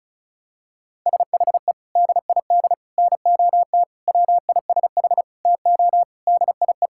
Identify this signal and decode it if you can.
Morse code, heard as a keyed tone, about 34 wpm, 700 Hz. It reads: SHE DID NOT WISH TO DIE
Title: Sentences from Top 1600 Words 35wpm